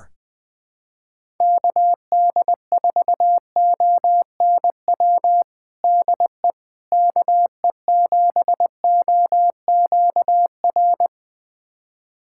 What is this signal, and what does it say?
Morse code, heard as a keyed tone, about 20 wpm, 700 Hz. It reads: KD4ONW DE KE7OQR
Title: USA Callsign DE Callsign - 6 Characters - 20wpm